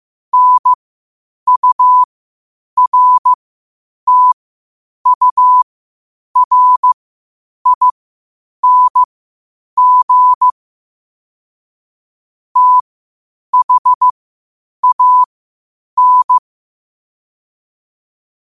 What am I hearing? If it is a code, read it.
Morse code, heard as a keyed tone, about 15 wpm, spaced as for 8 wpm, 1000 Hz. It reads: NURTURING THAN
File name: qotdfiles / 8wpm